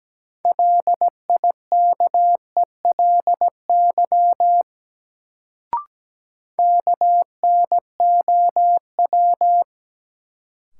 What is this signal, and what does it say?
Morse code, heard as a keyed tone, about 17 wpm, 700 Hz. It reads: LIKELY KNOW
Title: Sets of 2 Words - New Word Drill - 17wpm